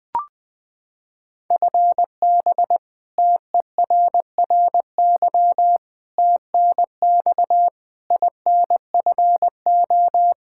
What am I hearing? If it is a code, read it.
Morse code, heard as a keyed tone, about 21 wpm, 700 Hz. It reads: FB TERRY TNX INFO
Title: Practice QSO - Exchange 3 - More Info - 20wpm